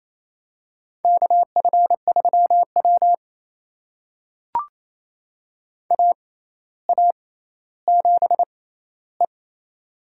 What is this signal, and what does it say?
Morse code, heard as a keyed tone, about 28 wpm, spaced as for 14 wpm, 700 Hz. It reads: KF3W AA7E
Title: USA Call Signs - 4 Characters - 6x Character Spacing - 28wpm